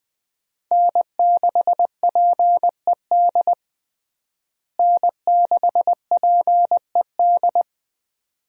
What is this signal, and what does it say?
Morse code, heard as a keyed tone, about 20 wpm, 700 Hz. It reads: N6PED N6PED
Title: USA Call Signs - 5 Characters - Repeated 2x - 20wpm